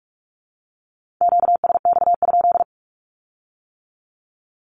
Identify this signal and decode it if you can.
Morse code, heard as a keyed tone, about 46 wpm, 700 Hz. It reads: QSX?
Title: Q-Codes with Spoken Meaning - 45wpm